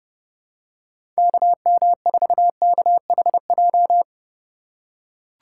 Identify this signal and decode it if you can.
Morse code, heard as a keyed tone, about 30 wpm, 700 Hz. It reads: KM4KHJ
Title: USA Call Signs - 6 Characters - 2x Character Spacing - 30wpm